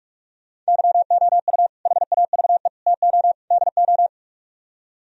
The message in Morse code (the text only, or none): YOU HAVE TO DO